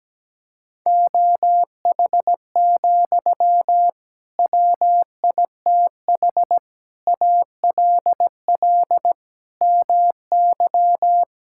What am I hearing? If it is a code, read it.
Morse code, heard as a keyed tone, about 17 wpm, 700 Hz. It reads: OH, WITH ALL MY
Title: Sentences from Top 1600 Words 17wpm